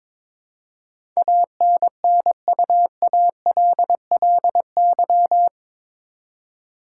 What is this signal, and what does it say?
Morse code, heard as a keyed tone, about 22 wpm, 700 Hz. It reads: ANNUALLY
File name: practice / 22wpm / Eight Letter Words - 22wpm